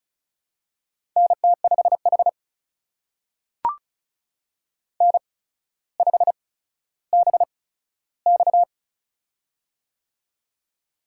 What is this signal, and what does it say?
Morse code, heard as a keyed tone, about 36 wpm, spaced as for 15 wpm, 700 Hz. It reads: NT5H N5BX
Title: USA Call Signs - 4 Characters - 8x Character Spacing - 35wpm